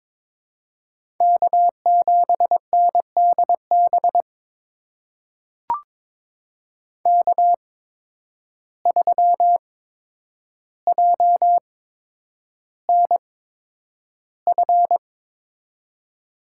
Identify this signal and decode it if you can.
Morse code, heard as a keyed tone, about 23 wpm, spaced as for 10 wpm, 700 Hz. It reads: K7NDB K3JNF
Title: USA Call Signs - 5 Characters - 8x Character Spacing - 22wpm